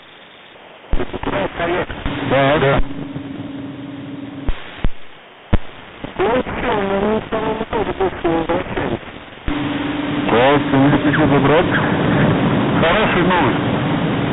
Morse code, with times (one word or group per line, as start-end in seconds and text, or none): none